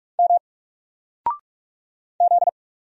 Morse code, heard as a keyed tone, about 44 wpm, 700 Hz. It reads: M Z
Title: Single Letter-Number 45wpm